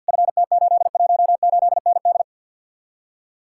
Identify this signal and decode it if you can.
Morse code, heard as a keyed tone, about 66 wpm, spaced as for 57 wpm, 700 Hz. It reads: FT818ND
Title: Speed-Racing Common QSO Phrases - 50wpm